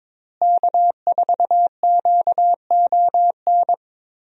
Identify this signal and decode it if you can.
Morse code, heard as a keyed tone, about 22 wpm, 700 Hz. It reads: K4QON
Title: USA Call Signs - 5 Characters - Repeated 3x - 22wpm